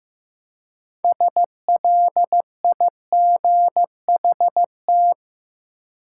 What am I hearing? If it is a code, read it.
Morse code, heard as a keyed tone, about 15 wpm, 700 Hz. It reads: SLIGHT
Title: Six Letter Words - 15wpm